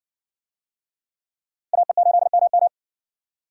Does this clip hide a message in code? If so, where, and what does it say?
Yes: Morse code from 1.7 to 2.7 s, heard as a keyed tone, about 60 wpm, 700 Hz. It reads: AE8MM